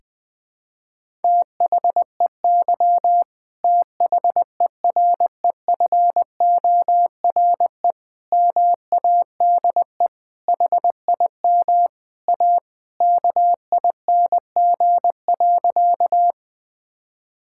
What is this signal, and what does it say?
Morse code, heard as a keyed tone, about 20 wpm, 700 Hz. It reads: THEY THEREFORE MADE HIM A KING.
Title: Sentences from Top 1000 Words - Review Part 5 - 20wpm